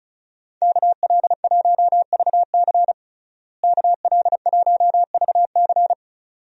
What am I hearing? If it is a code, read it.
Morse code, heard as a keyed tone, about 36 wpm, 700 Hz. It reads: KL1VC KL1VC
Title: USA Call Signs - 5 Characters - Repeated 2x - 35wpm